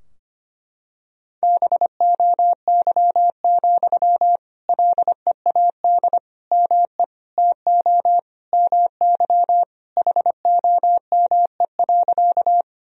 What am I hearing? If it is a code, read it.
Morse code, heard as a keyed tone, about 25 wpm, 700 Hz. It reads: BOY, LEAD ME TO MY HOME.
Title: Sentences from Top 1000 Words - Review Part 2 - 25wpm